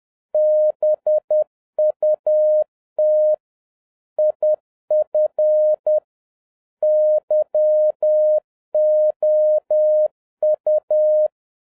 Morse code, heard as a keyed tone, about 10 wpm, 610 Hz. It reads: BUT IF YOU